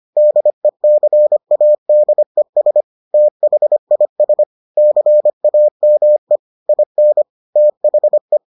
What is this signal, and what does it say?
Morse code, heard as a keyed tone, about 25 wpm, 600 Hz. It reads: DECADES THIS CAME IN THE